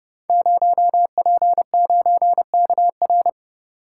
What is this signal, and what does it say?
Morse code, heard as a keyed tone, about 30 wpm, 700 Hz. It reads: 0P9KR
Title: Code Groups - 5 Letter:Numbers - 30wpm